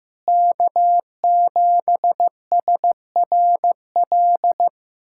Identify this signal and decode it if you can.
Morse code, heard as a keyed tone, about 15 wpm, 690 Hz. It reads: K7SRL
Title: SOTA QSO Practice 15wpm